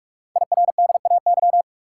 Morse code, heard as a keyed tone, about 44 wpm, 700 Hz. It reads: IRDAY
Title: Code Groups - 5 Letters - 45wpm